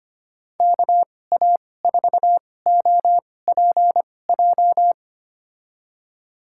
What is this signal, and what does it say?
Morse code, heard as a keyed tone, about 25 wpm, spaced as for 18 wpm, 700 Hz. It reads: KA4OPJ
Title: USA Call Signs - 6 Characters - 2x Character Spacing - 25wpm